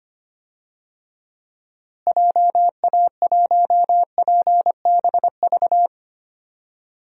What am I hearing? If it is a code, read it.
Morse code, heard as a keyed tone, about 25 wpm, 700 Hz. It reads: JA1PBV
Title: DXCC-100 Call Signs - 6 Characters - Repeated 2x - 25wpm